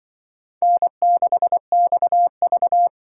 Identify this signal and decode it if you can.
Morse code, heard as a keyed tone, about 23 wpm, 700 Hz. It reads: N6XV
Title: Speed Racing - USA 1x2 Call Signs 20wpm